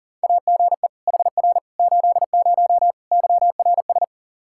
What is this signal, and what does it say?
Morse code, heard as a keyed tone, about 40 wpm, 700 Hz. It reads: AGE HR 80 YRS
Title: K1USN SST QSO Practice 40wpm